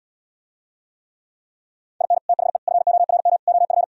Morse code, heard as a keyed tone, about 77 wpm, 700 Hz. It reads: IN THE FORM OF